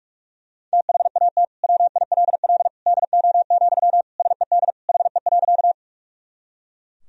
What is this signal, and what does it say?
Morse code, heard as a keyed tone, about 41 wpm, 700 Hz. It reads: THAT WILL DO, SED HE.
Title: Sentences from Top 100 Words and Abbreviations 45wpm